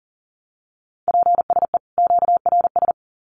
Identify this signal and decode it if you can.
Morse code, heard as a keyed tone, about 40 wpm, 700 Hz. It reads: PSE QRS
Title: Common QSO Phrases 40wpm